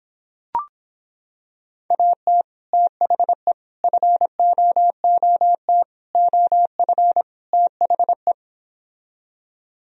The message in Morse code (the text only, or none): AT THE FOOT OF THE